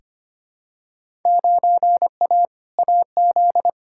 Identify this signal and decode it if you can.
Morse code, heard as a keyed tone, about 25 wpm, 700 Hz. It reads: 9A AZ
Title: Field Day Exchanges - 25wpm